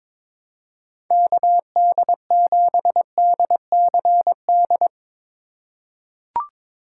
Morse code, heard as a keyed tone, about 22 wpm, 700 Hz. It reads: KD7DCD
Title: USA Call Signs - 6 Characters - 4x Character Spacing - 22wpm